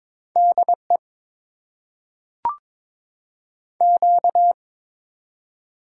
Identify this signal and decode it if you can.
Morse code, heard as a keyed tone, about 21 wpm, 700 Hz. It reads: DE Q
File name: Top 90 QSO Elements - 8x Character Spacing - 22wpm